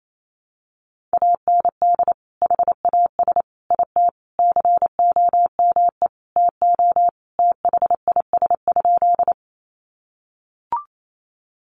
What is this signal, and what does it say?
Morse code, heard as a keyed tone, about 28 wpm, 700 Hz. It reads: AND HAS IT COME TO THIS?
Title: Sentences from Top 100 Words 28wpm